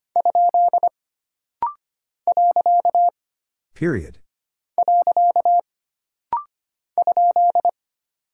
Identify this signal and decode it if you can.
Morse code, heard as a keyed tone, about 25 wpm, 700 Hz. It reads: ? . ?
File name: Single Punctuation-Prosign - Rapid-Fire - 25wpm